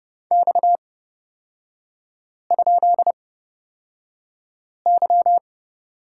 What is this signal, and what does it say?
Morse code, heard as a keyed tone, about 31 wpm, spaced as for 5 wpm, 700 Hz. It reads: X?Y
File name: Lesson 37e - Letter x - Three Characters 30wpm